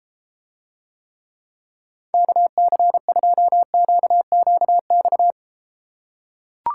Morse code, heard as a keyed tone, about 33 wpm, 700 Hz. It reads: KC2QQX